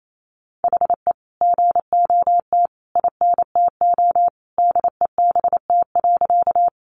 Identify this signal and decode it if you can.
Morse code, heard as a keyed tone, about 28 wpm, 700 Hz. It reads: HE GOT INTO DEBT.
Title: Sentences from Top 1300 Words 28wpm